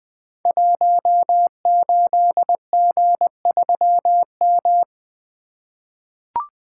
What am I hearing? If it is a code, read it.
Morse code, heard as a keyed tone, about 21 wpm, 700 Hz. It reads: 18G3M
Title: Code Groups - 5 Letter:Numbers - 20wpm